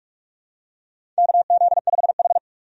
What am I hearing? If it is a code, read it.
Morse code, heard as a keyed tone, about 46 wpm, 700 Hz. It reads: KZ5H
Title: USA Call Signs - 4 Characters - 45wpm